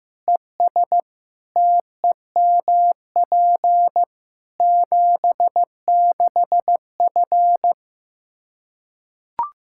Morse code, heard as a keyed tone, about 15 wpm, 700 Hz. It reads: ES TEMP 76F
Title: K1USN SST QSO Practice 15wpm